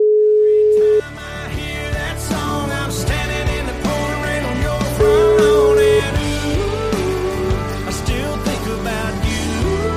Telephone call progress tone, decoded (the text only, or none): ringback tone